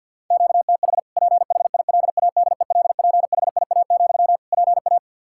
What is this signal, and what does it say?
Morse code, heard as a keyed tone, about 49 wpm, 700 Hz. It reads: QTH PHILADELPHIA, PA
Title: Practice QSO - Exchange 1 - Beginning - 50wpm